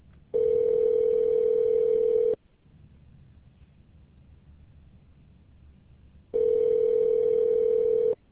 Ringback tone (North America), 440 + 480 Hz.